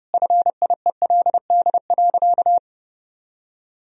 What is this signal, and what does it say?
Morse code, heard as a keyed tone, about 30 wpm, 700 Hz. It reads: FIELD.
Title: Sentences from Top 1000 Words - Review Part 2 - 30wpm